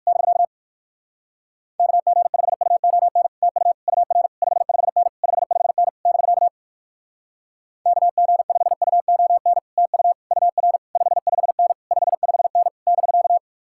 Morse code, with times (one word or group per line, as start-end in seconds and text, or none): 0.1-0.4 <BK>
1.8-3.3 KG5UON
3.4-3.7 TU
3.9-4.3 UR
4.4-5.1 55N
5.2-5.9 55N
6.1-6.5 <BK>
7.9-9.6 KG5UON
9.8-10.1 TU
10.3-10.8 UR
11.0-11.7 55N
11.9-12.7 55N
12.9-13.4 <BK>